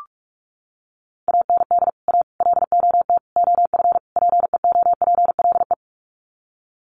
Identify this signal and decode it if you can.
Morse code, heard as a keyed tone, about 44 wpm, 700 Hz. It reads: AND A LOT OF PEOPLE
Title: Sets of 5 Words 45wpm